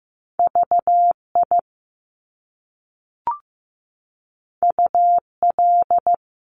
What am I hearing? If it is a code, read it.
Morse code, heard as a keyed tone, about 15 wpm, 700 Hz. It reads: VI UL
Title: Top Two Letter Combinations in Words 15wpm